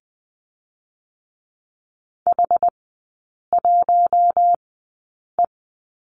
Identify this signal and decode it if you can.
Morse code, heard as a keyed tone, about 20 wpm, spaced as for 8 wpm, 700 Hz. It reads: H1E
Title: 074 Lesson 13e - Letter L - Three Characters 20wpm